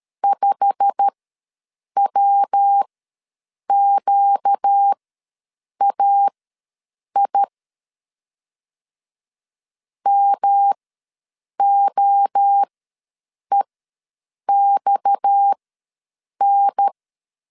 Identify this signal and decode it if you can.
Morse code, heard as a keyed tone, about 13 wpm, spaced as for 7 wpm, 800 Hz. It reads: 5WQAI MOEXN
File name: LEZIONI / 8 YJ45 / 8 YJ45 ws 12wpm 3146